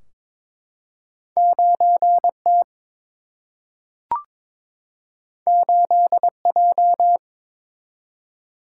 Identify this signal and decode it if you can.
Morse code, heard as a keyed tone, about 22 wpm, 700 Hz. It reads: 9 T 8 J
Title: Single-Letter Single-Number 22wpm